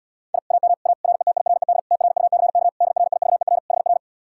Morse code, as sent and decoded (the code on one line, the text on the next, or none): .. / .- -- / .- / .-. - .. .-. -.. / - .-. ..- -.-. -.- / -.. .-. .. ...- . .-. / .... .-.
I AM A RETIRED TRUCK DRIVER HR